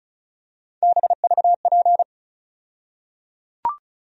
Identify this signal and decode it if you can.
Morse code, heard as a keyed tone, about 36 wpm, 700 Hz. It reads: BVP
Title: USA Call Sign Postfixes - 3 Letters 35wpm